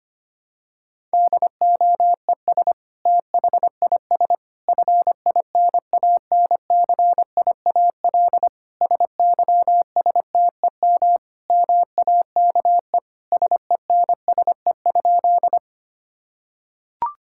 Morse code, heard as a keyed tone, about 25 wpm, 700 Hz. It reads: DOES THIS FINANCIAL SYSTEM MAKE SENSE?
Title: Sentences from Top 1000 Words - Review Part 5 - 25wpm